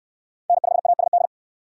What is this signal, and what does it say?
Morse code, heard as a keyed tone, about 64 wpm, 700 Hz. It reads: N5ASD